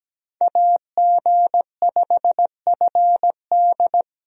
Morse code, heard as a keyed tone, about 17 wpm, 700 Hz. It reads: AG5FD
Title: USA Call Signs - 5 Characters - Repeated 2x - 17wpm